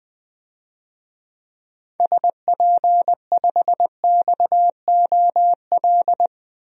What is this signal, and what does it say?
Morse code, heard as a keyed tone, about 20 wpm, 700 Hz. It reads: SP5XOL
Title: DXCC-100 Call Signs - 6 Characters - Repeated 3x - 20wpm